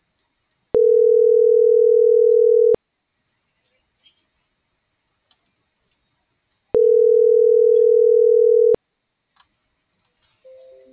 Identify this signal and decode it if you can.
Ringback tone (North America), 440 + 480 Hz.